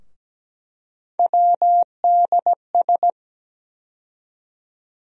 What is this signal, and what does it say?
Morse code, heard as a keyed tone, about 17 wpm, 700 Hz. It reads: WDS
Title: USA Call Sign Postfixes - 3 Letters 17wpm